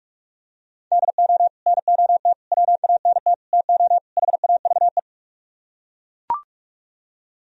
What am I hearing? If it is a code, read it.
Morse code, heard as a keyed tone, about 41 wpm, 700 Hz. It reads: DO NOT WANT TO HAVE